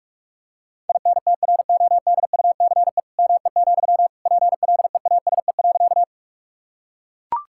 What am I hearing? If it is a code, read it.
Morse code, heard as a keyed tone, about 44 wpm, 700 Hz. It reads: INTRODUCE ME, PLEASE.
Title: Sentences from Top 1600 Words 45wpm